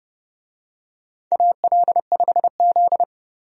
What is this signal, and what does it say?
Morse code, heard as a keyed tone, about 30 wpm, 700 Hz. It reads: AL5Z
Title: USA Call Signs - 4 Characters - 30wpm